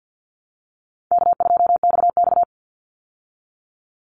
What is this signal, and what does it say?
Morse code, heard as a keyed tone, about 50 wpm, 700 Hz. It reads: K2XX
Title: Call Signs Repeated 2x - 50wpm